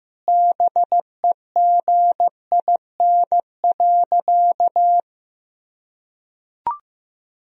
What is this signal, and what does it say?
Morse code, heard as a keyed tone, about 15 wpm, 700 Hz. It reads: BEGIN.
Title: Sentences from Top 300 Words and Abbreviations 15wpm